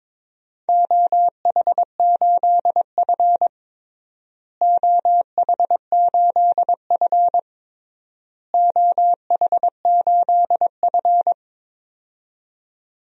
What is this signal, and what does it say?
Morse code, heard as a keyed tone, about 22 wpm, 700 Hz. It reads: OH8F OH8F OH8F
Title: DXCC-100 Call Signs - 4 Characters - Repeated 3x - 22wpm